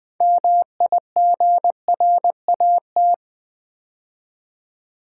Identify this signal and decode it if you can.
Morse code, heard as a keyed tone, about 20 wpm, 700 Hz. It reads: MIGRAT
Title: Build A Word - 3 Letters At A Time - 9-Letter Words - 20wpm